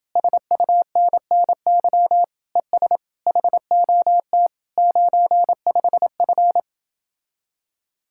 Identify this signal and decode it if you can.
Morse code, heard as a keyed tone, about 32 wpm, 700 Hz. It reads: SUNNY ES HOT 95F